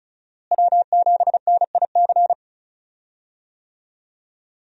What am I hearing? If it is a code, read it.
Morse code, heard as a keyed tone, about 36 wpm, 700 Hz. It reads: W7NIC
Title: USA Call Signs - 5 Characters - Repeated 3x - 35wpm